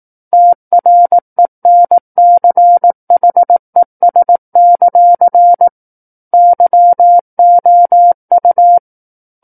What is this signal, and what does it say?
Morse code, heard as a keyed tone, about 18 wpm, 700 Hz. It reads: TRENCHES; YOU
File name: 18 WPM